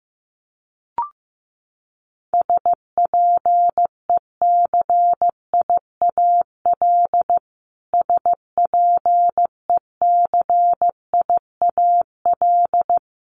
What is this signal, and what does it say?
Morse code, heard as a keyed tone, about 15 wpm, 700 Hz. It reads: SPECIAL SPECIAL
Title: Top 500 Words Repeated 15wpm